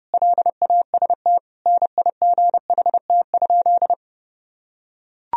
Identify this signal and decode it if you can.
Morse code, heard as a keyed tone, about 30 wpm, 700 Hz. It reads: LAST NIGHT?
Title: Sentences from Top 1000 Words - Review Part 4 - 30wpm